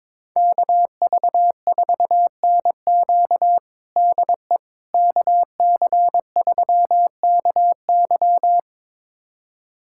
Morse code, heard as a keyed tone, about 22 wpm, 700 Hz. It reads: KV4NQ DE KC3KY
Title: USA Callsign DE Callsign 22wpm